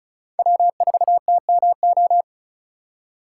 Morse code, heard as a keyed tone, about 36 wpm, 700 Hz. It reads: W4TMO
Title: K1USN SST QSO Practice - 35wpm